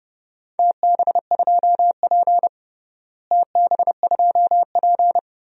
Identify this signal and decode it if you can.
Morse code, heard as a keyed tone, about 30 wpm, 700 Hz. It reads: TB2P TB2P
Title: DXCC-100 Call Signs - 4 Characters - Repeated 2x - 30wpm